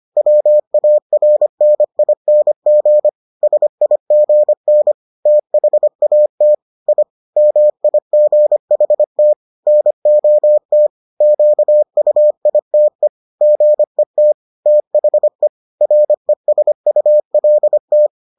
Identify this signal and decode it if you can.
Morse code, heard as a keyed tone, about 25 wpm, 600 Hz. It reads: WARNING SIGN THAT I MIGHT NOT QUITE GET THE RESULT